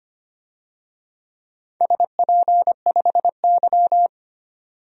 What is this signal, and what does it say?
Morse code, heard as a keyed tone, about 25 wpm, 700 Hz. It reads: SP5Y
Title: DXCC-100 Call Signs - 4 Characters - 25wpm